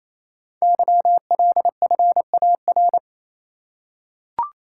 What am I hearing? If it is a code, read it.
Morse code, heard as a keyed tone, about 28 wpm, 700 Hz. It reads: YLFAR
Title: Code Groups - 5 Letters - 28wpm